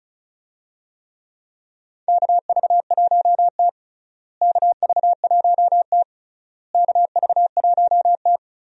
Morse code, heard as a keyed tone, about 36 wpm, 700 Hz. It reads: KV1T KV1T KV1T
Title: USA Call Signs - 4 Characters - Repeated 3x - 35wpm